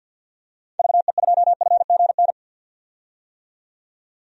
Morse code, heard as a keyed tone, about 50 wpm, 700 Hz. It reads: VE2FGN
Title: DXCC-100 Call Signs - 6 Characters - 50wpm